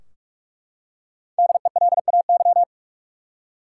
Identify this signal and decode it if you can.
Morse code, heard as a keyed tone, about 43 wpm, 700 Hz. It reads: DELAY
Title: Five Letter Words - 45wpm